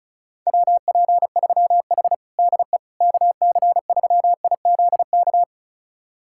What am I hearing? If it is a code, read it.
Morse code, heard as a keyed tone, about 34 wpm, 700 Hz. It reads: WP3H DE KC3IZK
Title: USA Callsign DE Callsign 35wpm